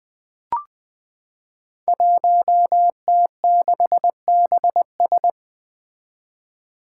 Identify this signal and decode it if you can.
Morse code, heard as a keyed tone, about 20 wpm, 700 Hz. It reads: 1T6BS
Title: Five Challenging Letter-Number Combinations 20wpm